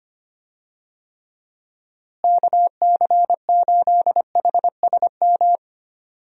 Morse code, heard as a keyed tone, about 25 wpm, 700 Hz. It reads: KC8HSM
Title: USA Call Signs - 6 Characters - Repeated 2x - 25wpm